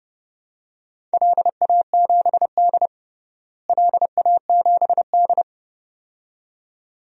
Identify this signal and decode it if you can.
Morse code, heard as a keyed tone, about 30 wpm, 700 Hz. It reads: LA7D LA7D
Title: DXCC-100 Call Signs - 4 Characters - Repeated 2x - 30wpm